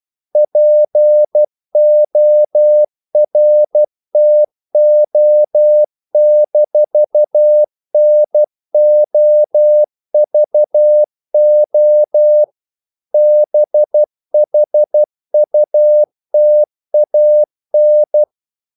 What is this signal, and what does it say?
Morse code, heard as a keyed tone, about 12 wpm, 600 Hz. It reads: PORTO-NOVO BHUTAN
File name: CW / Practice / NationalCapitals12